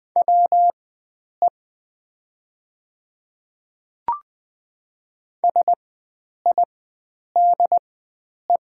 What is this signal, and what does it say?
Morse code, heard as a keyed tone, about 20 wpm, spaced as for 9 wpm, 700 Hz. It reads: WE SIDE
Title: 4x Character Spacing - Top 300 Words - 20wpm